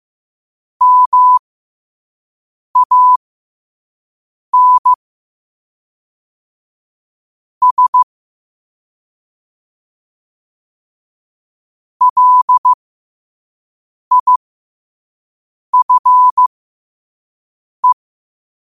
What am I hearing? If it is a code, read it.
Morse code, heard as a keyed tone, about 15 wpm, spaced as for 5 wpm, 1000 Hz. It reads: MAN S LIFE
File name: qotdfiles / 5wpm